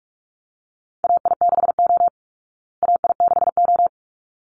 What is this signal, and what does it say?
Morse code, heard as a keyed tone, about 44 wpm, 700 Hz. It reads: AI6O AI6O
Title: Call Signs Repeated 3x - 45wpm